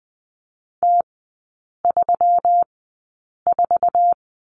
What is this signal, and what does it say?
Morse code, heard as a keyed tone, about 20 wpm, spaced as for 8 wpm, 700 Hz. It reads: T34
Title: 127 Lesson 21e - Number 6 - Three Characters 20wpm